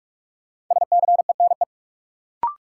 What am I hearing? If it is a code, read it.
Morse code, heard as a keyed tone, about 44 wpm, 700 Hz. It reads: SCENE
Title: Sentences from Top 1100 Words - New Word Drill Alt - 45wpm